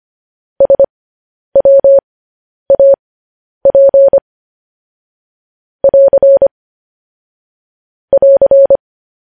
Morse code, heard as a keyed tone, about 25 wpm, spaced as for 10 wpm, 550 Hz. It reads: SWAP <AR> <AR>
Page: news headlines in morse code at 10 wpm